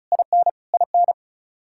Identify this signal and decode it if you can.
Morse code, heard as a keyed tone, about 34 wpm, 700 Hz. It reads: IN IN
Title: Practice QSO - Exchange 1 - Beginning - 35wpm